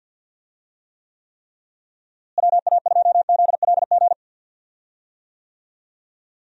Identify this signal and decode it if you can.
Morse code, heard as a keyed tone, about 48 wpm, 700 Hz. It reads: WA2ZLG
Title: USA Call Signs - 6 Characters - Repeated 3x - 50wpm